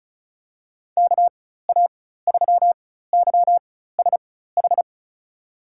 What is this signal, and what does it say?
Morse code, heard as a keyed tone, about 36 wpm, spaced as for 16 wpm, 700 Hz. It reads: KA3YSH
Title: USA Call Signs - 6 Characters - 4x Character Spacing - 35wpm